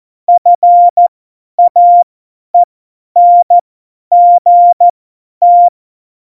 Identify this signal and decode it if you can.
Morse code, heard as a keyed tone, about 14 wpm, spaced as for 10 wpm, 700 Hz. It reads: FAENGT
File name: Morsekurs / Witze / Witz 14. August 2018, 50 F70